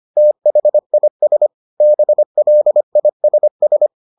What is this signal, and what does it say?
Morse code, heard as a keyed tone, about 25 wpm, 600 Hz. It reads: THIS BLISS